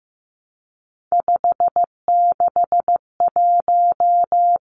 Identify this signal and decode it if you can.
Morse code, heard as a keyed tone, about 15 wpm, 700 Hz. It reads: 561
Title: Numbers - 3-Digit 15wpm